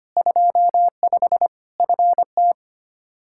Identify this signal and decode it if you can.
Morse code, heard as a keyed tone, about 25 wpm, 700 Hz. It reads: 25 FT
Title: Practice QSO - Exchange 2 - Info - 25wpm